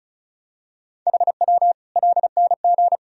Morse code, heard as a keyed tone, about 36 wpm, 700 Hz. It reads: HW LNG
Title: Sentences from Top 200 Words and Abbreviations 35wpm